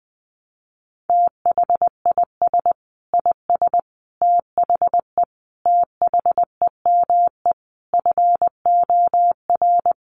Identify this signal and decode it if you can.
Morse code, heard as a keyed tone, about 20 wpm, 700 Hz. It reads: THIS IS THE THEME FOR
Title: Sentences from Top 1100 Words 20wpm